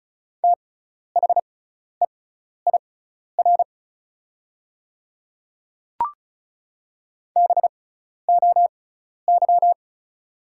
Morse code, heard as a keyed tone, about 36 wpm, spaced as for 12 wpm, 700 Hz. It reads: THEIR BOY
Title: 6x Character Spacing - Top 300 Words - 35wpm